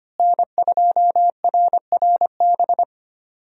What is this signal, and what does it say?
Morse code, heard as a keyed tone, about 25 wpm, 700 Hz. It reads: N2RRB
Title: USA Call Signs - 5 Characters - Repeated 3x - 25wpm